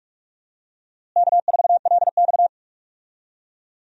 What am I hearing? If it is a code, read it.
Morse code, heard as a keyed tone, about 42 wpm, 700 Hz. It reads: K4LX
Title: USA Call Signs - 4 Characters - Repeated 3x - 45wpm